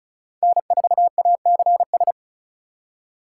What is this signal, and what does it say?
Morse code, heard as a keyed tone, about 36 wpm, 700 Hz. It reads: N4ACS
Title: USA Call Signs - 5 Characters - 3x Character Spacing - 35wpm